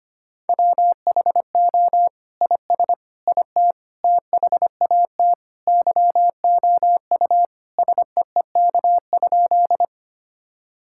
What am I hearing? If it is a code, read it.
Morse code, heard as a keyed tone, about 25 wpm, 700 Hz. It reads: WHO IS IT THAT YOU SEEK?
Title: Sentences from Top 1600 Words 25wpm